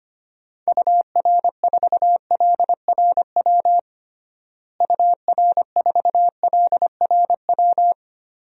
Morse code, heard as a keyed tone, about 25 wpm, 700 Hz. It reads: UR4LRW UR4LRW
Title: DXCC-100 Call Signs - 6 Characters - Repeated 3x - 25wpm